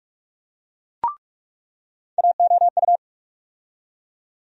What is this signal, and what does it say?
Morse code, heard as a keyed tone, about 46 wpm, 700 Hz. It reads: AOU
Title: USA Call Sign Postfixes - 3 Letters 45wpm